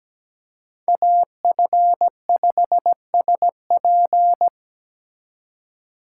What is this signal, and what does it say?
Morse code, heard as a keyed tone, about 20 wpm, 700 Hz. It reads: AF5SP
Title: Speed Racing - USA 2x2 Call Signs 17wpm